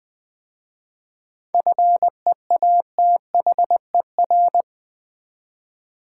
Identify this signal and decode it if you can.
Morse code, heard as a keyed tone, about 20 wpm, 700 Hz. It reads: FEATHER
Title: 7 Letter Words - First 3 Letters Given - 20wpm